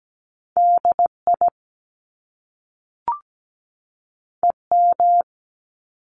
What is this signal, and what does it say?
Morse code, heard as a keyed tone, about 17 wpm, 700 Hz. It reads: DI EM
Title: Top Two Letter Combinations in Words 17wpm